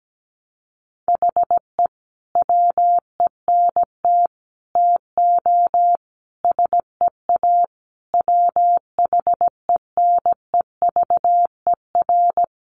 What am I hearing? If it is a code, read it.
Morse code, heard as a keyed tone, about 17 wpm, 700 Hz. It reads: HE WENT TO SEA WHENEVER
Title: Sentences from Top 1200 Words 17wpm